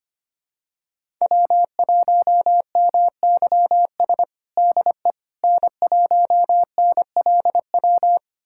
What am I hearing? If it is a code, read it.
Morse code, heard as a keyed tone, about 25 wpm, 700 Hz. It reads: W1MYS DE N1NLW
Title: USA Callsign DE Callsign - 5 Characters - 25wpm